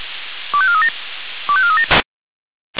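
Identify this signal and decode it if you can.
Five-tone selective call (ZVEI), heard as quick runs of five tones, tones of about 70 ms. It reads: 25437 25437
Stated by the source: ZVEI five-tone selective call received via SDR.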